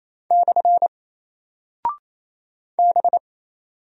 Morse code, heard as a keyed tone, about 27 wpm, 700 Hz. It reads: / B
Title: Single Character (Letter:Number:Punctuation:Prosign) 28wpm